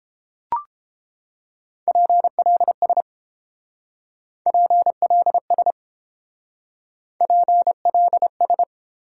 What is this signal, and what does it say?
Morse code, heard as a keyed tone, about 28 wpm, 700 Hz. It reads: PLS PLS PLS
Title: Speed-Racing Top 90 QSO Elements 22wpm